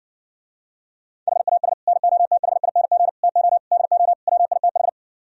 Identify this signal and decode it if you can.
Morse code, heard as a keyed tone, about 70 wpm, 700 Hz. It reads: HAS NOTHING TO DO WITH